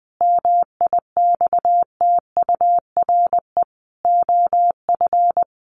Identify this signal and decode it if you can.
Morse code, heard as a keyed tone, about 20 wpm, 700 Hz. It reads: MIXTURE OF
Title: Sentences from Top 1500 Words 20wpm